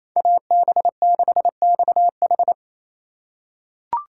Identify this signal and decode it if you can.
Morse code, heard as a keyed tone, about 28 wpm, 700 Hz. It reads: AB6XH
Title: USA Call Signs - 5 Characters - 2x Character Spacing - 28wpm